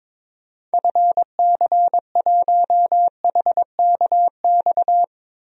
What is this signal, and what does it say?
Morse code, heard as a keyed tone, about 22 wpm, 700 Hz. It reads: FC1HKX
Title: DXCC-100 Call Signs - 6 Characters - 22wpm